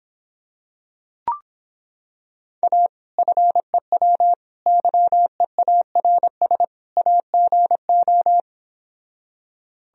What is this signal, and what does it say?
Morse code, heard as a keyed tone, about 26 wpm, 700 Hz. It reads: A FEW YEARS AGO